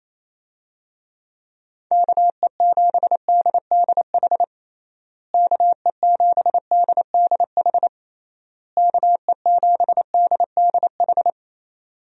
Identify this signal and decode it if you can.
Morse code, heard as a keyed tone, about 27 wpm, 700 Hz. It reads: KE7DDH KE7DDH KE7DDH
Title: USA Call Signs - 6 Characters - Repeated 3x - 28wpm